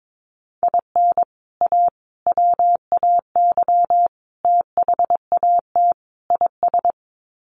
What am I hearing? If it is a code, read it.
Morse code, heard as a keyed tone, about 22 wpm, 700 Hz. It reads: IN A WAY THAT IS
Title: Sets of 5 Words Encore 22wpm